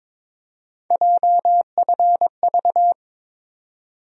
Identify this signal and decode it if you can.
Morse code, heard as a keyed tone, about 22 wpm, 700 Hz. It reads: JFV
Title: USA Call Sign Postfixes - 3 Letters 22wpm